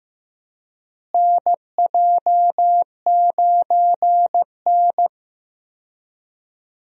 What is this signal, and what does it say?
Morse code, heard as a keyed tone, about 16 wpm, 700 Hz. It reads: NJ9N
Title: USA Call Signs - 4 Characters - Repeated 2x - 15wpm